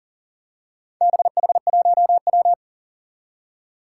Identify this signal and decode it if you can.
Morse code, heard as a keyed tone, about 40 wpm, 700 Hz. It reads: BH1W
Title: Four Challenging Letter-Number Combinations 40wpm